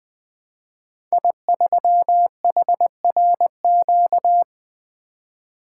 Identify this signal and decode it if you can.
Morse code, heard as a keyed tone, about 20 wpm, 700 Hz. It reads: I3HRQ
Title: Code Groups - 5 Letter:Numbers - 20wpm